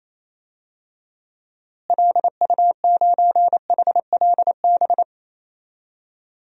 Morse code, heard as a keyed tone, about 28 wpm, 700 Hz. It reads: LU9HLB
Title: DXCC-100 Call Signs - 6 Characters - Repeated 2x - 28wpm